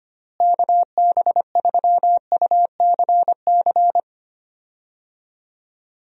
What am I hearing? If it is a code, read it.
Morse code, heard as a keyed tone, about 25 wpm, 700 Hz. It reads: KB3UCC
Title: USA Call Signs - 6 Characters - Repeated 3x - 25wpm